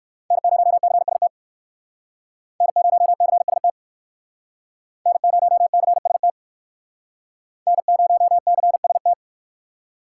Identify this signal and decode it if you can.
Morse code, heard as a keyed tone, about 59 wpm, 700 Hz. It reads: N0CST N0CST N0CST N0CST